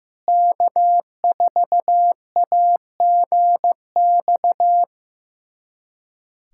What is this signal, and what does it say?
Morse code, heard as a keyed tone, about 15 wpm, 700 Hz. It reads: K4AGX
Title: USA Callsign DE Callsign 15wpm